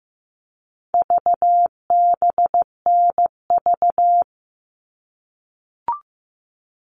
Morse code, heard as a keyed tone, about 15 wpm, 700 Hz. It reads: VBNV
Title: All Cut Numbers - 4-Digit 15wpm